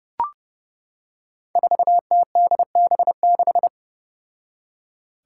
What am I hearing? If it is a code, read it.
Morse code, heard as a keyed tone, about 30 wpm, 700 Hz. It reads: 4TDB6
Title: Five Challenging Letter-Number Combinations 30wpm